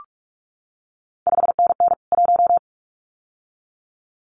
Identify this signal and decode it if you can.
Morse code, heard as a keyed tone, about 46 wpm, 700 Hz. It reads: 5NN 1
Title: CQ WW DX Contest Exchange Encore - 45wpm